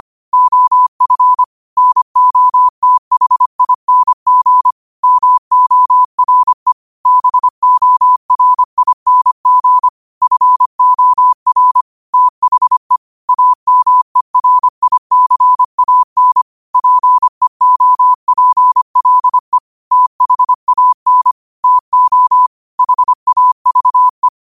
Morse code, heard as a keyed tone, about 25 wpm, 1000 Hz. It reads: OF NOTHING MORE BORING FOR THE AMERICAN PEOPLE THAN TO HAVE